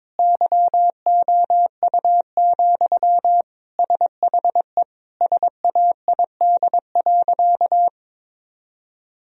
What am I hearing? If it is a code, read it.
Morse code, heard as a keyed tone, about 22 wpm, 700 Hz. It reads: YOU, SHE SAID.
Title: Sentences from Top 1000 Words - Review Part 1 - 22wpm